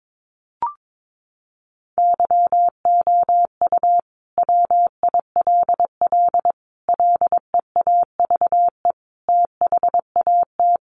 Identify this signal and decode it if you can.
Morse code, heard as a keyed tone, about 22 wpm, 700 Hz. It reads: YOU WILL LEAVE THAT
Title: Sentences from Top 900 Words 22wpm